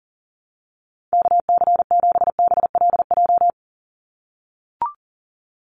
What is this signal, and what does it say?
Morse code, heard as a keyed tone, about 40 wpm, 700 Hz. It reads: KC7BLJ
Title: USA Call Signs 40wpm